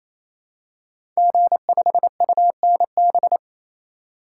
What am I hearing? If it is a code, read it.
Morse code, heard as a keyed tone, about 28 wpm, 700 Hz. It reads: G5UNB
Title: Code Groups - 5 Letter:Numbers - 28wpm